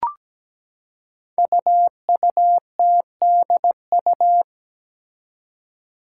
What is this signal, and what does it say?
Morse code, heard as a keyed tone, about 17 wpm, 700 Hz. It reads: UUTDU
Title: Code Groups - 5 Letters - 17wpm